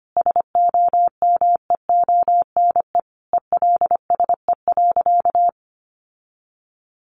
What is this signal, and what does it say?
Morse code, heard as a keyed tone, about 25 wpm, 700 Hz. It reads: SOMEONE ELSE.
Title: Sentences from Top 800 Words 25wpm